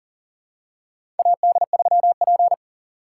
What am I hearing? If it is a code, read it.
Morse code, heard as a keyed tone, about 40 wpm, 700 Hz. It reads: AD3P
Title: USA Call Signs - 4 Characters - 6x Character Spacing - 40wpm